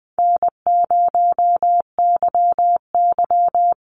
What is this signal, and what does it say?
Morse code, heard as a keyed tone, about 20 wpm, 700 Hz. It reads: N0YY
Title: Easy Call Signs Repeated 3x - 20wpm